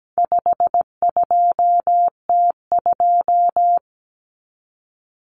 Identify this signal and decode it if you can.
Morse code, heard as a keyed tone, about 17 wpm, 700 Hz. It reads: 52T2
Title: Common Cut Numbers - 4-Digit 17wpm